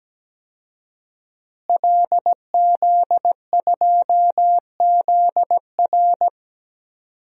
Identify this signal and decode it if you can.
Morse code, heard as a keyed tone, about 17 wpm, 700 Hz. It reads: LZ2ZR
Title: DXCC-100 Call Signs - 5 Characters - Repeated 3x - 17wpm